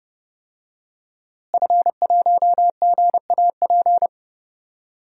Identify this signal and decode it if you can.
Morse code, heard as a keyed tone, about 30 wpm, 700 Hz. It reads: F1GAP
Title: DXCC-100 Call Signs - 5 Characters - Repeated 3x - 30wpm